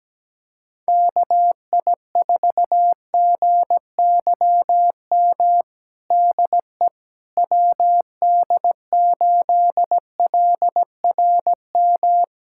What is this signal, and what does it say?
Morse code, heard as a keyed tone, about 17 wpm, 700 Hz. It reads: KI4GYM DE WD8LRM
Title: USA Callsign DE Callsign - 6 Characters - 17wpm